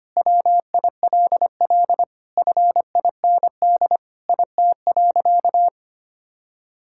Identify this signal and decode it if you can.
Morse code, heard as a keyed tone, about 25 wpm, 700 Hz. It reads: WILL FIND IT.
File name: practice / 25wpm / Sentences with 4-Word Phrases - 25wpm